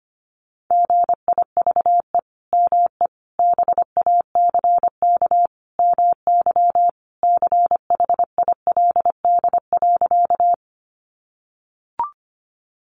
Morse code, heard as a keyed tone, about 25 wpm, 700 Hz. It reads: GIVE ME BACK MY CHILD.
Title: Sentences from Top 800 Words 25wpm